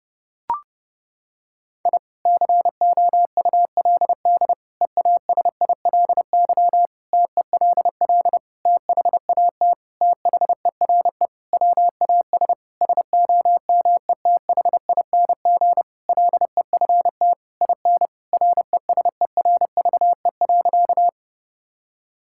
Morse code, heard as a keyed tone, about 30 wpm, 700 Hz. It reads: I COULD EASILY TELL THAT THERE WAS SOMETHING LEFT IN RESERVE.
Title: Sentences from Top 1000 Words - Review Part 6 - 30wpm